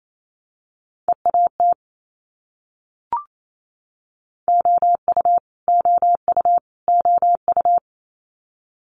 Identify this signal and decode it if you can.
Morse code, heard as a keyed tone, about 28 wpm, 700 Hz. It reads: EAT OU OU OU
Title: Multi-letter Phonemes Repeated 3x 28wpm